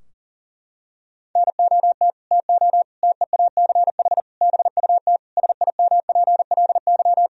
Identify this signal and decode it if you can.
Morse code, heard as a keyed tone, about 37 wpm, 700 Hz. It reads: NOT TO TEACH BUT SIMPLY
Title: Sentences from Top 1000 Words - Review Part 4 - 40wpm